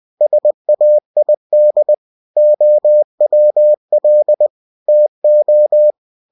Morse code, heard as a keyed tone, about 20 wpm, 600 Hz. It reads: SAID OWL TO
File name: cw-books / winnie-the-pooh / winnie-20wpm-page-0081